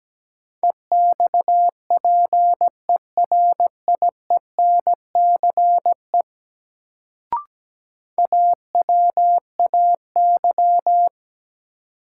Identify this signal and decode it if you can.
Morse code, heard as a keyed tone, about 17 wpm, 700 Hz. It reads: EXPERIENCE AWAY
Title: Sentences from Top 600 Words - New Word Drill - 17wpm